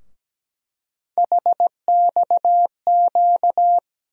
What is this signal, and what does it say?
Morse code, heard as a keyed tone, about 17 wpm, 700 Hz. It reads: HXQ
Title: USA Call Sign Postfixes - 3 Letters 17wpm